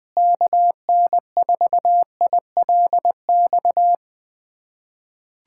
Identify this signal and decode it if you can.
Morse code, heard as a keyed tone, about 21 wpm, 700 Hz. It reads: KN4ILX
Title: USA Callsign DE Callsign - 6 Characters - 20wpm